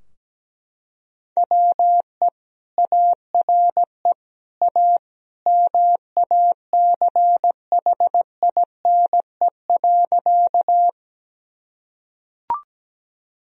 Morse code, heard as a keyed tone, about 17 wpm, 700 Hz. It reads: WE ARE A MACHINE.
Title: Sentences from Top 1000 Words - Review Part 2 - 17wpm